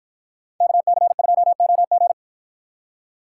Morse code, heard as a keyed tone, about 53 wpm, 700 Hz. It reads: KC2OG